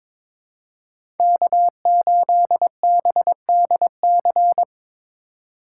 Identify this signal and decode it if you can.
Morse code, heard as a keyed tone, about 22 wpm, 700 Hz. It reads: K8BDC
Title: USA Call Signs - 5 Characters - 4x Character Spacing - 22wpm